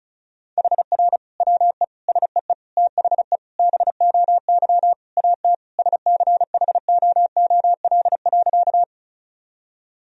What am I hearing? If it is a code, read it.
Morse code, heard as a keyed tone, about 36 wpm, 700 Hz. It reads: HR WE SEE THE BOY AT SCHOOL.
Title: Sentences from Top 300 Words and Abbreviations 35wpm